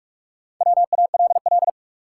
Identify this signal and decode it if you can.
Morse code, heard as a keyed tone, about 42 wpm, 700 Hz. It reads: WALL
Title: Sentences from Top 1000 Words - Review Part 6 - New Word Drill - 45wpm